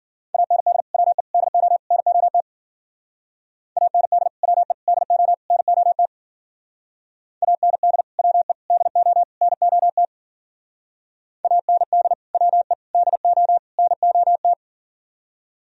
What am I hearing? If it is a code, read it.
Morse code, heard as a keyed tone, about 51 wpm, 700 Hz. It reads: AND WE DO NOT AND WE DO NOT AND WE DO NOT AND WE DO NOT